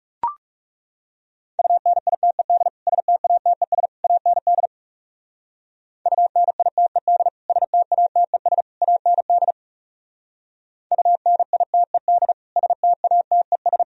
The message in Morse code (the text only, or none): UNITED STATES AND UNITED STATES AND UNITED STATES